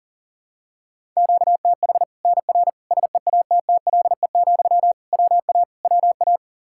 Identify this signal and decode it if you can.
Morse code, heard as a keyed tone, about 40 wpm, 700 Hz. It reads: QTH NR SEATTLE, WA WA
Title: Practice QSO - Exchange 1 - Beginning - 40wpm